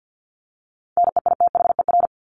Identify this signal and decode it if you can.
Morse code, heard as a keyed tone, about 50 wpm, 700 Hz. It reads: NEITHER
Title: Top 1501-1750 Words 50wpm